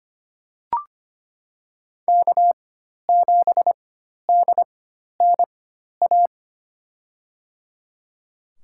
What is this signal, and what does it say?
Morse code, heard as a keyed tone, about 25 wpm, spaced as for 12 wpm, 700 Hz. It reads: K7DNA
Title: USA Call Signs - 5 Characters - 4x Character Spacing - 25wpm